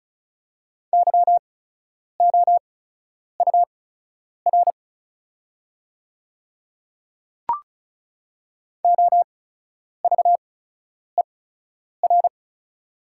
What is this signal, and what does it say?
Morse code, heard as a keyed tone, about 36 wpm, spaced as for 10 wpm, 690 Hz. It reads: YOUR OVER
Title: 8x Character Spacing - Top 300 Words - 35wpm